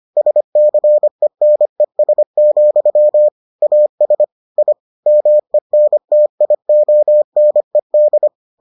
Morse code, heard as a keyed tone, about 25 wpm, 600 Hz. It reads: SCENES, AS I MENTIONED